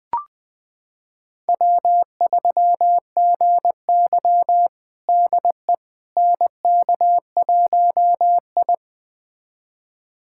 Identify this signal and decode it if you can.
Morse code, heard as a keyed tone, about 20 wpm, 700 Hz. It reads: W3GY DE NK1I
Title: USA Callsign DE Callsign - 4 Characters - 20wpm